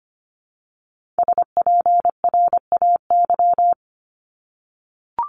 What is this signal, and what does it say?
Morse code, heard as a keyed tone, about 25 wpm, 700 Hz. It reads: SPRAY
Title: Multi-letter Phonemes 25wpm